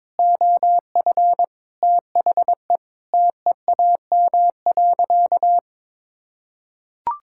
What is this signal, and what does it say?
Morse code, heard as a keyed tone, about 22 wpm, 700 Hz. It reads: OF THE TEAM.
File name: practice / 22wpm / Sentences with 3-Word Phrases - 22wpm